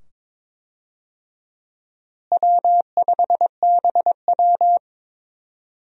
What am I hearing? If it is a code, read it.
Morse code, heard as a keyed tone, about 22 wpm, 700 Hz. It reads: W5BW
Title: USA Call Signs - 4 Characters - Repeated 2x - 22wpm